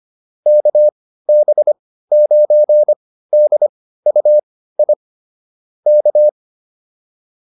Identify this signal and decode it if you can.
Morse code, heard as a keyed tone, about 25 wpm, spaced as for 15 wpm, 600 Hz. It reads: KB9DUI K
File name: QSO308 15